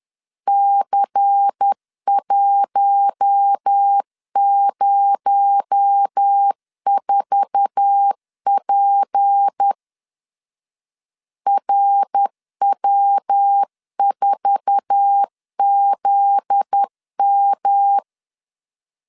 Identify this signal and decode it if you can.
Morse code, heard as a keyed tone, about 10 wpm, 800 Hz. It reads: C104P RW4ZM
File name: LEZIONI / 8 YJ45 / 8 YJ4 10wpm 2633